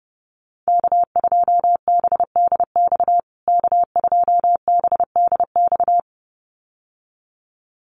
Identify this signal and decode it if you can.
Morse code, heard as a keyed tone, about 30 wpm, 700 Hz. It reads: K2BDX K2BDX
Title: USA Call Signs Repeated 2x 30wpm